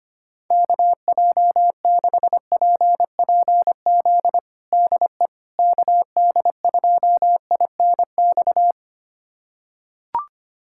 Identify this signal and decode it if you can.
Morse code, heard as a keyed tone, about 25 wpm, 700 Hz. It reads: KJ6PPZ DE KD2INX
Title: USA Callsign DE Callsign 25wpm